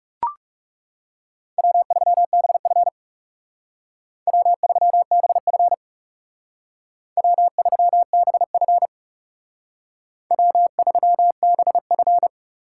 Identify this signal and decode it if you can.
Morse code, heard as a keyed tone, about 36 wpm, 700 Hz. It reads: W3BF W3BF W3BF W3BF
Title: Speed Racing - USA 1x2 Call Signs 30wpm